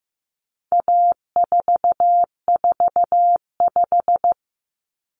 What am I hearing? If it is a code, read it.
Morse code, heard as a keyed tone, about 15 wpm, 700 Hz. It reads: A445
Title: Common Cut Numbers - 4-Digit 15wpm